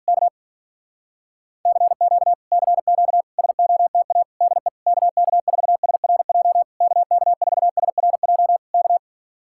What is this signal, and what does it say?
Morse code, heard as a keyed tone, about 45 wpm, 700 Hz. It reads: K CQ CQ SOTA DE KK4SRJ KK4SRJ K